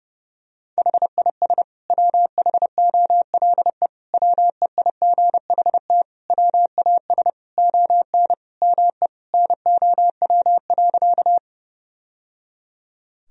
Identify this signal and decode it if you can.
Morse code, heard as a keyed tone, about 30 wpm, 700 Hz. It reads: HIS WHOLE WEIGHT WAS ON ME NOW.
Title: Sentences from Top 1000 Words - Review Part 3 - 30wpm